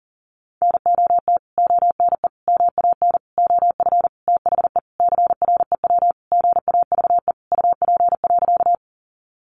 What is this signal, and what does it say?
Morse code, heard as a keyed tone, about 40 wpm, 700 Hz. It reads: NOT ONE MAN OF THE CREW GAVE UP.
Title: Sentences from Top 1500 Words 40wpm